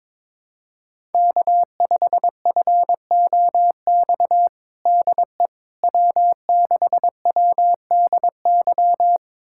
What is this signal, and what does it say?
Morse code, heard as a keyed tone, about 21 wpm, 700 Hz. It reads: K5FOX DE W6WDY
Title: USA Callsign DE Callsign - 5 Characters - 22wpm